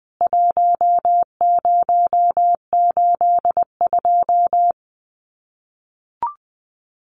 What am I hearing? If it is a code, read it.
Morse code, heard as a keyed tone, about 20 wpm, 700 Hz. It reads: 1082
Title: CWT Exchange 20wpm